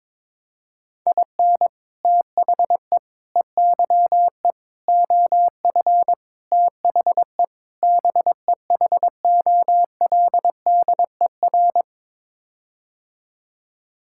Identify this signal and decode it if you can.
Morse code, heard as a keyed tone, about 22 wpm, 700 Hz. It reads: IN THE EYE OF THE BEHOLDER
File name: practice / 22wpm / Sets of 6 Words - 22wpm